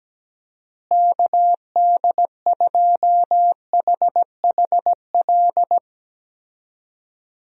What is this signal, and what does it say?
Morse code, heard as a keyed tone, about 17 wpm, 700 Hz. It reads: KD2HHL
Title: USA Call Signs - 6 Characters - Repeated 3x - 17wpm